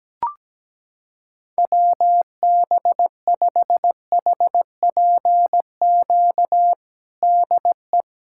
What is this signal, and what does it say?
Morse code, heard as a keyed tone, about 17 wpm, 700 Hz. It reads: WB5HPQ DE
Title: USA Callsign DE Callsign - 6 Characters - 17wpm